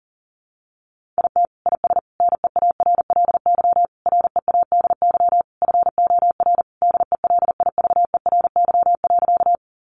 Morse code, heard as a keyed tone, about 37 wpm, 700 Hz. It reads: IT IS NEARLY READY FOR DELIVERY.
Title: Sentences from Top 1500 Words 40wpm